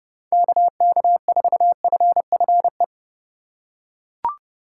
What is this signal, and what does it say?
Morse code, heard as a keyed tone, about 29 wpm, 700 Hz. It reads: KK4FFE